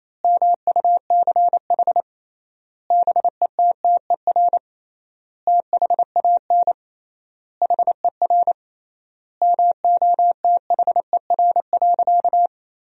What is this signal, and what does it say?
Morse code, heard as a keyed tone, about 28 wpm, 700 Hz. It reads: MUCH BETTER THAN HER MOTHER.
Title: 3x Word-Spacing - Sentences from Top 400 Words - 28wpm